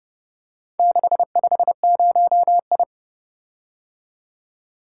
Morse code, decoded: 650I

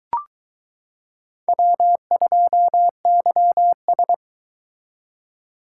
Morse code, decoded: W2YS

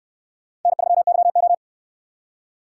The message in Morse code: N3YO

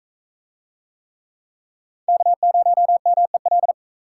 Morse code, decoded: K0MEL

